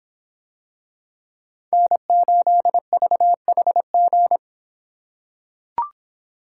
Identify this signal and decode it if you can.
Morse code, heard as a keyed tone, about 26 wpm, 700 Hz. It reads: N8VHG